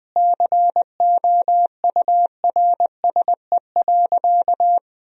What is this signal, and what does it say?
Morse code, heard as a keyed tone, about 21 wpm, 700 Hz. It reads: COURSE.
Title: 2x Word-Spacing - Sentences from Top 500 Words - 20wpm